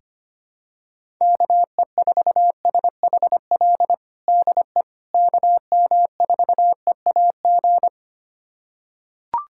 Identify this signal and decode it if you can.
Morse code, heard as a keyed tone, about 25 wpm, 700 Hz. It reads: KE4SHL DE KM4EAG
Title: USA Callsign DE Callsign - 6 Characters - 25wpm